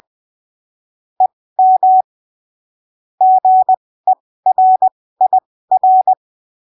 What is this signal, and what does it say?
Morse code, heard as a keyed tone, about 20 wpm, spaced as for 15 wpm, 750 Hz. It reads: EM GERIR
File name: arquivos / texto / texto 06 20ppm